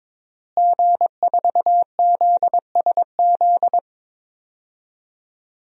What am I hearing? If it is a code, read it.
Morse code, heard as a keyed tone, about 22 wpm, 700 Hz. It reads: G4ZSZ
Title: DXCC-100 Call Signs - 5 Characters - Repeated 3x - 22wpm